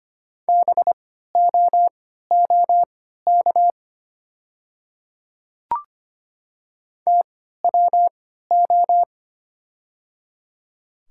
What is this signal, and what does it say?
Morse code, heard as a keyed tone, about 25 wpm, spaced as for 14 wpm, 700 Hz. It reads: BOOK TWO
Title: 3x Character Spacing - Top 300 Words - 25wpm